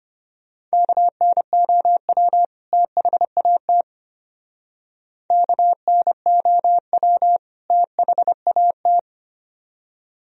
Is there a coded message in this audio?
Yes: Morse code from 0.7 to 9.0 s, heard as a keyed tone, about 31 wpm, 700 Hz. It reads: KNOW THAT KNOW THAT